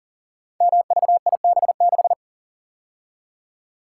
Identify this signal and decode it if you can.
Morse code, heard as a keyed tone, about 42 wpm, 690 Hz. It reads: MVIB6
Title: Code Groups - 5 Letter:Numbers - 40wpm